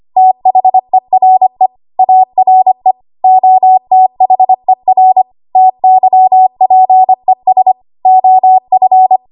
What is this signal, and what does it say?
Morse code, heard as a keyed tone, about 25 wpm, 750 Hz. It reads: THERE ARE OTHER TYPES OF
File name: Morse / 25WPM